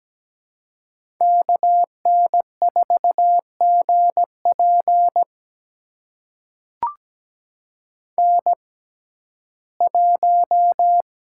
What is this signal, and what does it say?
Morse code, heard as a keyed tone, about 17 wpm, spaced as for 12 wpm, 700 Hz. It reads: KN4GP N1
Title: USA Call Signs - 5 Characters - 6x Character Spacing - 17wpm